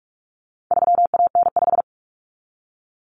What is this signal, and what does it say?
Morse code, heard as a keyed tone, about 44 wpm, 700 Hz. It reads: 3AN5
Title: Common Cut Numbers - 4-Digit 45wpm